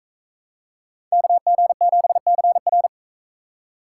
Morse code, heard as a keyed tone, about 40 wpm, 700 Hz. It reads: KG7CR